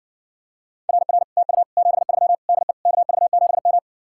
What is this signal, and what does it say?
Morse code, heard as a keyed tone, about 60 wpm, 700 Hz. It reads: RR TU 73 DE KV7M